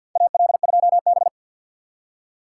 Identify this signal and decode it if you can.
Morse code, heard as a keyed tone, about 51 wpm, 700 Hz. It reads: AL2B
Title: USA Call Signs - 4 Characters - 3x Character Spacing - 50wpm